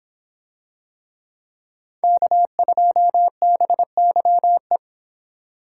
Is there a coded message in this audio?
Yes: Morse code from 2.0 to 4.8 s, heard as a keyed tone, about 26 wpm, 700 Hz. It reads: K2BYE